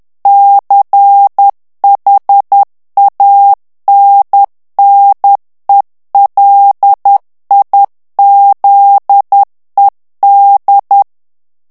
Morse code, heard as a keyed tone, about 10 wpm, 800 Hz. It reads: CHANNELIZED
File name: ANGLAIS texte clair 3 vitesse 10 mots minute